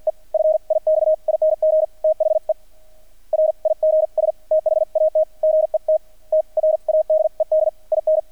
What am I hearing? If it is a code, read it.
Morse code, heard as a keyed tone, about 50 wpm, spaced as for 43 wpm, 630 Hz. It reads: E FIXING THE RIGS THAT GET TRADED IN
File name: archived / CRSnet